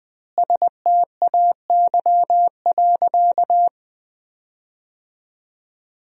Morse code, heard as a keyed tone, about 20 wpm, 700 Hz. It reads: STAY.
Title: 2x Word-Spacing - Sentences from Top 500 Words - 20wpm